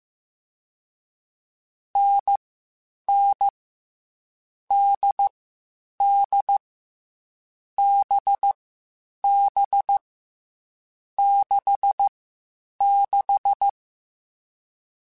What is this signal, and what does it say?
Morse code, heard as a keyed tone, about 15 wpm, 800 Hz. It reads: N N D D B B 6 6
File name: Short Runs / Short Runs-mp3s / NDB NDB6 @13